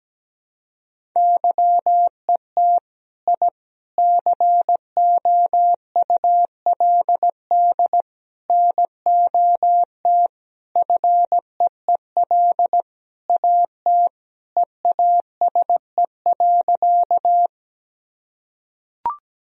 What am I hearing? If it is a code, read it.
Morse code, heard as a keyed tone, about 17 wpm, 700 Hz. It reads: YET I COULD NOT FEEL AT EASE.
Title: Sentences from Top 1000 Words - Review Part 1 - 17wpm